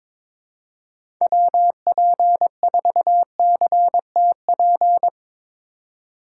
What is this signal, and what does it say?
Morse code, heard as a keyed tone, about 22 wpm, 700 Hz. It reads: WP4CTP
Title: USA Call Signs - 6 Characters - 6x Character Spacing - 22wpm